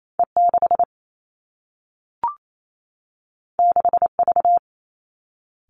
Morse code, heard as a keyed tone, about 28 wpm, 700 Hz. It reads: E 6 6 V
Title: Two Challenging Letter-Number Combinations 28wpm